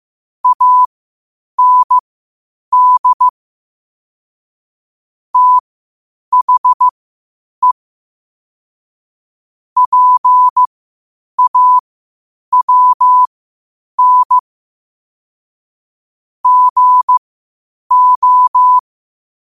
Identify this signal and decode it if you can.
Morse code, heard as a keyed tone, about 15 wpm, spaced as for 8 wpm, 1000 Hz. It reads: AND THE PAWN GO
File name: qotdfiles / 8wpm